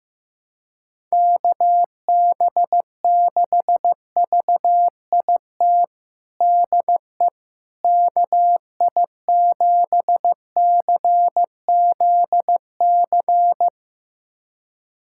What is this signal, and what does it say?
Morse code, heard as a keyed tone, about 15 wpm, 700 Hz. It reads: KB6VIT DE KI7CZC
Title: USA Callsign DE Callsign - 6 Characters - 15wpm